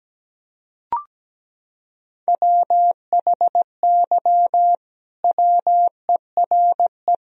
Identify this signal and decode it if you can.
Morse code, heard as a keyed tone, about 17 wpm, 700 Hz. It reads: WHY WERE
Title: Sentences from Top 1000 Words - Review Part 4 - 17wpm